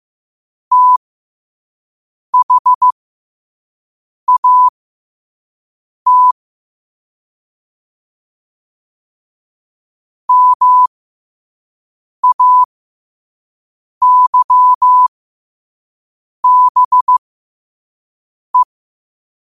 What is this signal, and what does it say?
Morse code, heard as a keyed tone, about 15 wpm, spaced as for 5 wpm, 1000 Hz. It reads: THAT MAYBE
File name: qotdfiles / 5wpm